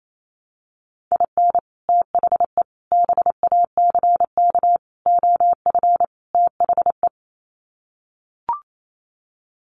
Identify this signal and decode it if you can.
Morse code, heard as a keyed tone, about 28 wpm, 700 Hz. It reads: IN THE BACK OF THE
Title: Sets of 5 Words 28wpm